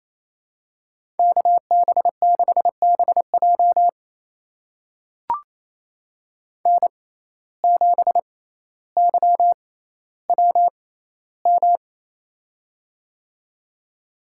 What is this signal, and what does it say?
Morse code, heard as a keyed tone, about 28 wpm, spaced as for 14 wpm, 700 Hz. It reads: KB6BJ N7YWM
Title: USA Call Signs - 6x Character Spacing - 28wpm